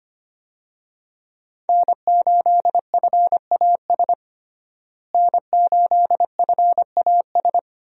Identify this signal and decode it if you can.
Morse code, heard as a keyed tone, about 25 wpm, 700 Hz. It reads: N8FAS N8FAS
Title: USA Call Signs - 5 Characters - Repeated 2x - 25wpm